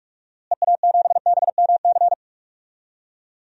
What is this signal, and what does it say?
Morse code, heard as a keyed tone, about 45 wpm, 690 Hz. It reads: EA7BMC
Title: DXCC-100 Call Signs - 6 Characters - Repeated 2x - 45wpm